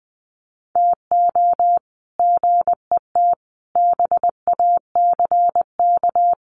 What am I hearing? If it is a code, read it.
Morse code, heard as a keyed tone, about 19 wpm, 700 Hz. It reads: TO GET BACK
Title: Sets of 5 Words Encore 20wpm